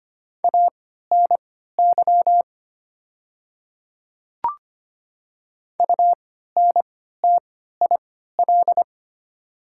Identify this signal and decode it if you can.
Morse code, heard as a keyed tone, about 25 wpm, spaced as for 14 wpm, 700 Hz. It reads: ANY UNTIL
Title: 3x Character Spacing - Top 300 Words - 25wpm